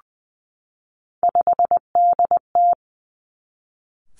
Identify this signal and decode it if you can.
Morse code, heard as a keyed tone, about 20 wpm, 700 Hz. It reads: 5DT
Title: Three Challenging Letter-Number Combinations 20wpm